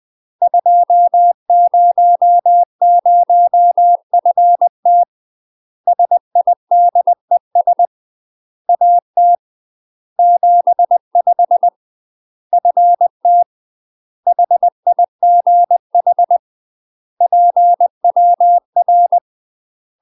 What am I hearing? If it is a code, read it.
Morse code, heard as a keyed tone, about 20 wpm, 700 Hz. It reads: 200FT SIDES AT 75 FT HIGH PWR
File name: UK MASTER QSOs TRAINING / 20WPM / UK MASTER QSOs 0001